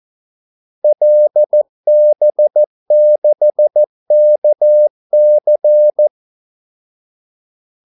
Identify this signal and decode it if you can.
Morse code, heard as a keyed tone, about 14 wpm, 600 Hz. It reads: LB6KC
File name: Callsigns-14wpm